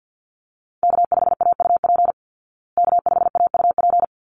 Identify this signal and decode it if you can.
Morse code, heard as a keyed tone, about 50 wpm, 700 Hz. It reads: K5AUP K5AUP
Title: Call Signs Repeated 3x - 50wpm